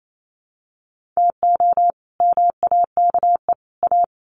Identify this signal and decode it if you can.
Morse code, heard as a keyed tone, about 28 wpm, 700 Hz. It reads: TO MAKE A
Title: Sets of 3 Words 28wpm